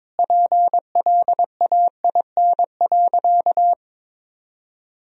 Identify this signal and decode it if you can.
Morse code, heard as a keyed tone, about 22 wpm, 700 Hz. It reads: PLAIN.
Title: Sentences from Top 1000 Words - Review Part 2 - 22wpm